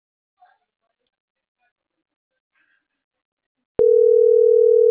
Ringback tone (North America), 440 + 480 Hz.